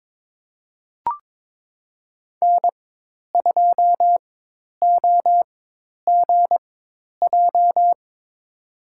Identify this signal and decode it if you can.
Morse code, heard as a keyed tone, about 23 wpm, spaced as for 10 wpm, 700 Hz. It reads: N2OGJ
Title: USA Call Signs - 5 Characters - 4x Character Spacing - 22wpm